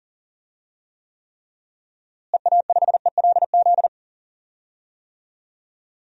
Morse code, heard as a keyed tone, about 37 wpm, 700 Hz. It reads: EA5ELZ
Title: DXCC-100 Call Signs - 6 Characters - Repeated 3x - 40wpm